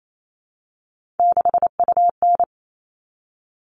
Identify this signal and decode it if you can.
Morse code, heard as a keyed tone, about 28 wpm, 700 Hz. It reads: 6UN
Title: All Cut Numbers - 3-Digit 28wpm